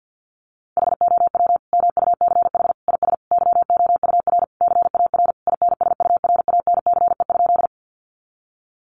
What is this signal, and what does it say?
Morse code, heard as a keyed tone, about 50 wpm, 700 Hz. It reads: HOW MUCH IS YOUR CAR INSURANCE?
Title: Sentences from Top 1100 Words 50wpm